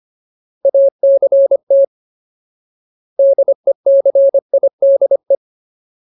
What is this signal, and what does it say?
Morse code, heard as a keyed tone, about 25 wpm, 550 Hz. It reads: ACT DECIDE
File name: Ultimate Word List / 25WPM Ultimate